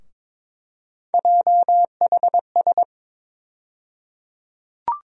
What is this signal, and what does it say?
Morse code, heard as a keyed tone, about 22 wpm, 700 Hz. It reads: JHS
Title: USA Call Sign Postfixes - 3 Letters 22wpm